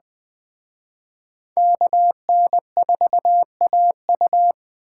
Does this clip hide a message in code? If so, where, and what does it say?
Yes: Morse code from 1.6 to 4.5 s, heard as a keyed tone, about 20 wpm, 700 Hz. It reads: KN4AU